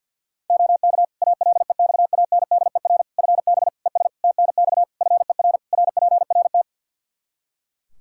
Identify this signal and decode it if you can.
Morse code, heard as a keyed tone, about 46 wpm, 700 Hz. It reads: OK ALEXANDER FB ES TNX FER RPRT
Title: K1USN SST QSO Practice 50wpm